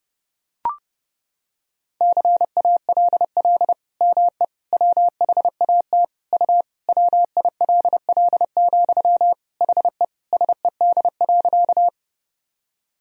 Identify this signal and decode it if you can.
Morse code, heard as a keyed tone, about 30 wpm, 700 Hz. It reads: CALL ME WHAT U WILL, HE SED.
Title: Sentences from Top 200 Words and Abbreviations 30wpm